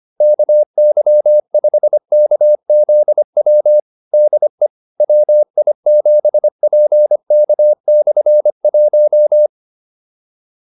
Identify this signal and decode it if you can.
Morse code, heard as a keyed tone, about 25 wpm, 600 Hz. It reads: KY5KZW DE WI7PK/1
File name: QSO304a 25